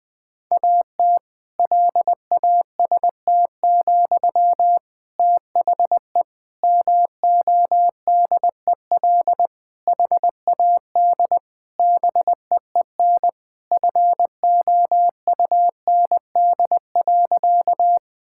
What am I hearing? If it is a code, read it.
Morse code, heard as a keyed tone, about 20 wpm, 700 Hz. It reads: AT LAST, THE MODEL HAD BEEN FOUND.
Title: Sentences from Top 1000 Words - Review Part 5 - 20wpm